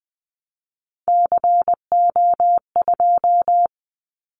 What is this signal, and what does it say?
Morse code, heard as a keyed tone, about 20 wpm, 700 Hz. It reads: CO2
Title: Callsign Prefixes for Top 100 DXCC - 20wpm